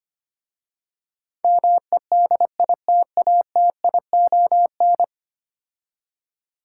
Morse code, heard as a keyed tone, about 25 wpm, 700 Hz. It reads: MEDITATION
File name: 10 Letter Words - First 3 Letters Given - 25wpm